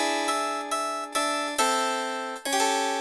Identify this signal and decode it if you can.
There is no code.